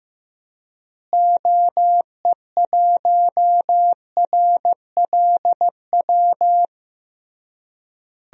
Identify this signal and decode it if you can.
Morse code, heard as a keyed tone, about 15 wpm, 700 Hz. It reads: OE1RLW
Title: DXCC-100 Call Signs - 6 Characters - Repeated 3x - 15wpm